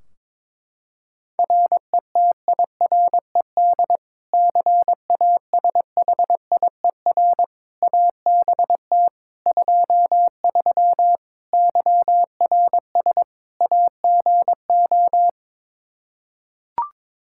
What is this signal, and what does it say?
Morse code, heard as a keyed tone, about 21 wpm, 700 Hz. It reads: RETIRED CASHIER ABT 23 YRS AGO
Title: Practice QSO - Exchange 3 - More Info - 22wpm